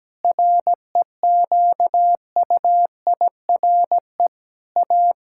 Morse code, heard as a keyed tone, about 17 wpm, 700 Hz. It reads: REQUIRE A
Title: Sentences from Top 1000 Words - Review Part 6 - 17wpm